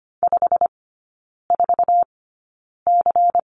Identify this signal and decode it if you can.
Morse code, heard as a keyed tone, about 25 wpm, spaced as for 9 wpm, 700 Hz. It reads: 54C
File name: course / 25:10wpm / 093 Lesson 16e - Letter c - Three Characters 25wpm